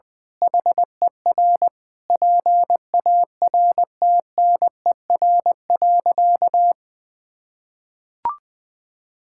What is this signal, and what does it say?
Morse code, heard as a keyed tone, about 20 wpm, 700 Hz. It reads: HER PARTNER.
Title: Sentences with 8-Word Phrases - Spoken Beginning - 20wpm